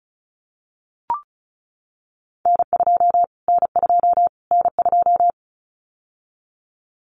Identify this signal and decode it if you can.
Morse code, heard as a keyed tone, about 34 wpm, 700 Hz. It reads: N2 N2 N2
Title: Callsign Prefixes for Top 25 DXCC - Repeated 3x - 35wpm